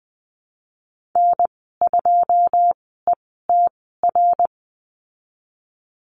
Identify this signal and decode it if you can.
Morse code, heard as a keyed tone, about 20 wpm, spaced as for 15 wpm, 700 Hz. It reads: N2ETR
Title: 088 Lesson 15g - Number 5 - Call Signs 20wpm